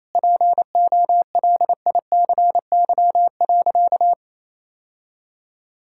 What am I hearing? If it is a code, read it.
Morse code, heard as a keyed tone, about 28 wpm, 700 Hz. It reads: POLICY.
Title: Sentences from Top 1600 Words 28wpm